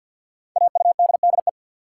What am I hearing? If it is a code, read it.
Morse code, heard as a keyed tone, about 51 wpm, 700 Hz. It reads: AUDDE
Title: Code Groups - 5 Letters - 50wpm